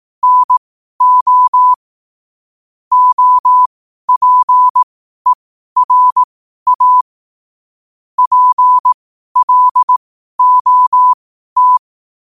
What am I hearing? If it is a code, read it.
Morse code, heard as a keyed tone, about 18 wpm, spaced as for 12 wpm, 1000 Hz. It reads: NO OPERA PLOT
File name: qotdfiles / 12wpm